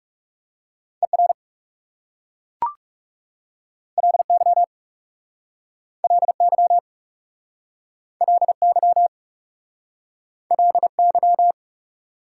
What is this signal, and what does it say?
Morse code, heard as a keyed tone, about 37 wpm, 700 Hz. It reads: ER LY LY LY LY
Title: Speed-Racing Prefixes and Suffixes 30wpm